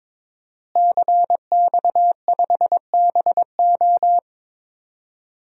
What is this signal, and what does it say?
Morse code, heard as a keyed tone, about 22 wpm, 700 Hz. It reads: CX5BO
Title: DXCC-100 Call Signs - 5 Characters - Repeated 3x - 22wpm